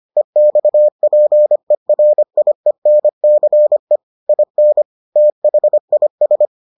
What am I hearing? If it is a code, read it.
Morse code, heard as a keyed tone, about 25 wpm, 600 Hz. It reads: EXPERIENCE IN THIS